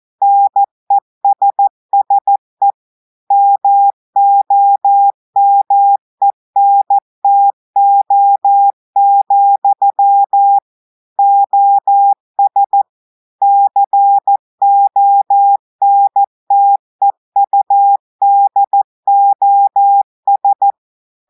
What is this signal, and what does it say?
Morse code, heard as a keyed tone, about 14 wpm, 800 Hz. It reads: NESSE MOMENTO, OS CONTEUDOS
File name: audios / portugais / portugais texte en clair 2 vitesse 14 mots minutes